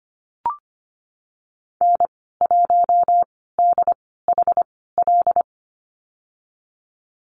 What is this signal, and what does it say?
Morse code, heard as a keyed tone, about 25 wpm, spaced as for 16 wpm, 700 Hz. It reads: N1DHL
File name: Lesson 15g - Number 5 - Call Signs 25wpm